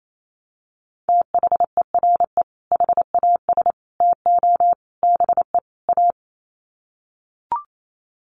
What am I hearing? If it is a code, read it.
Morse code, heard as a keyed tone, about 28 wpm, 700 Hz. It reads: THERE HAS TO BE A
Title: Sets of 5 Words Encore 28wpm